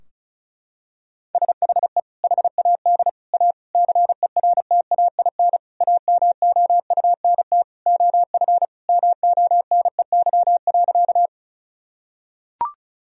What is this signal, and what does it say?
Morse code, heard as a keyed tone, about 36 wpm, 700 Hz. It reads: SHE HAD A CERTAIN AMOUNT OF MONEY.
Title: Sentences from Top 1000 Words - Review Part 4 - 35wpm